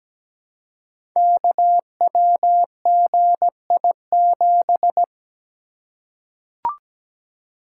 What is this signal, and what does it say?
Morse code, heard as a keyed tone, about 17 wpm, 700 Hz. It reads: KWGI7
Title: Code Groups - 5 Letter:Numbers - 17wpm